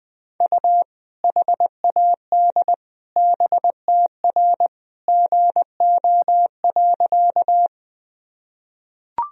Morse code, heard as a keyed tone, about 21 wpm, 700 Hz. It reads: U HAD BTR GO.
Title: Sentences from Top 400 Words and Abbreviations 20wpm